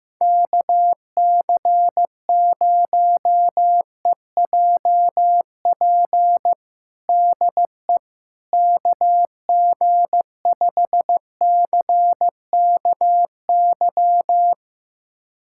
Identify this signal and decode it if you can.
Morse code, heard as a keyed tone, about 15 wpm, 700 Hz. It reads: KC0EJP DE KG5CKY
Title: USA Callsign DE Callsign - 6 Characters - 15wpm